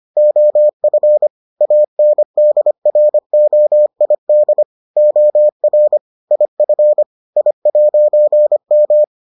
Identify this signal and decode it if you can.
Morse code, heard as a keyed tone, about 25 wpm, 600 Hz. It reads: OF ANDROID OR IF I'M